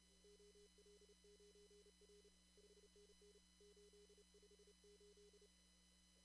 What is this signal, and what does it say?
Morse code, heard as a keyed tone, about 32 wpm, 410 Hz. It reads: OL9R 5NN 858
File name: CW / kt5j wpx cw-op-k5pi